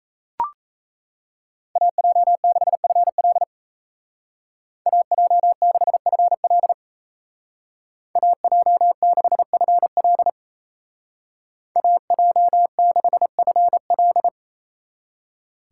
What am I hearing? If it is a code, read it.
Morse code, heard as a keyed tone, about 37 wpm, 700 Hz. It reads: AJ6FL AJ6FL AJ6FL AJ6FL